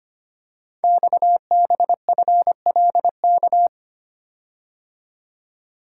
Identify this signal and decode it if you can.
Morse code, heard as a keyed tone, about 25 wpm, 700 Hz. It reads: XBFLK
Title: Code Groups - 5 Letters - 25wpm